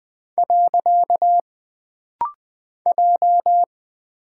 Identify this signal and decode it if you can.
Morse code, heard as a keyed tone, about 20 wpm, 700 Hz. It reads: . J
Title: Single Character (Letter:Number:Punctuation:Prosign) - Rapid-Fire - 20wpm